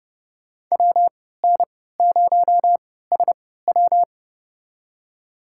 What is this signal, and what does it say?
Morse code, heard as a keyed tone, about 30 wpm, spaced as for 17 wpm, 700 Hz. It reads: WN0SW
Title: USA Call Signs - 5 Characters - 3x Character Spacing - 30wpm